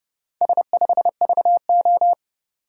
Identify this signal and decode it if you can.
Morse code, heard as a keyed tone, about 29 wpm, 700 Hz. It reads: S5VO